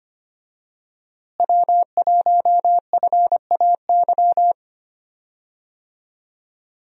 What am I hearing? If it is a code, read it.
Morse code, heard as a keyed tone, about 25 wpm, 700 Hz. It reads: W1FAY
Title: USA Call Signs - 5 Characters - Repeated 2x - 25wpm